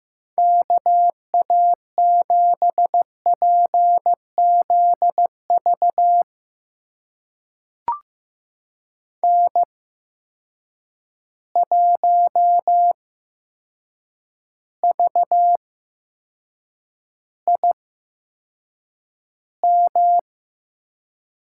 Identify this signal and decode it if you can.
Morse code, heard as a keyed tone, about 15 wpm, spaced as for 7 wpm, 700 Hz. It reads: KA7PZV N1VIM
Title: USA Call Signs - 8x Character Spacing - 15wpm